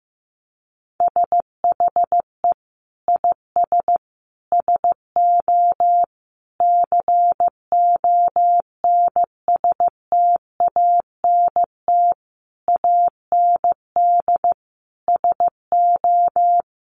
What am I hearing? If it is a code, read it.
Morse code, heard as a keyed tone, about 15 wpm, 700 Hz. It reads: SHE IS SO CONSTANT AND SO
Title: Sentences from Top 1500 Words 15wpm